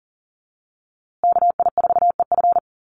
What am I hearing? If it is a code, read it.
Morse code, heard as a keyed tone, about 40 wpm, 700 Hz. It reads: KI4EF
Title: USA Call Signs Repeated 2x 40wpm